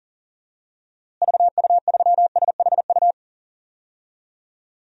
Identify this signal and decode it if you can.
Morse code, heard as a keyed tone, about 40 wpm, 700 Hz. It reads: VU3SHU
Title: DXCC-100 Call Signs - 6 Characters - Repeated 2x - 40wpm